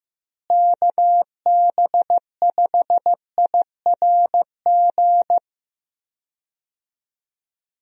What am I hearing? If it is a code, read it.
Morse code, heard as a keyed tone, about 15 wpm, 700 Hz. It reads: KB5IRG
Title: USA Call Signs - 6 Characters - Repeated 3x - 15wpm